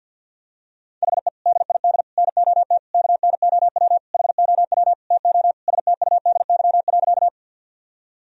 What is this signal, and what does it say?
Morse code, heard as a keyed tone, about 51 wpm, 700 Hz. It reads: HE DID NOT KNOW HOW TO STUDY.
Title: Sentences from Top 1000 Words - Review Part 1 - 50wpm